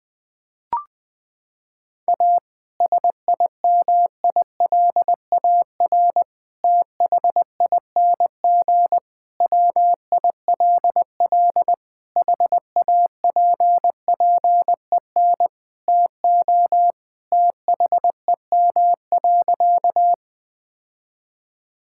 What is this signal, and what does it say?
Morse code, heard as a keyed tone, about 20 wpm, 700 Hz. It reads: A SIMILAR THING WILL HAPPEN TO THEM.
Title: Sentences from Top 1000 Words - Review Part 5 - 20wpm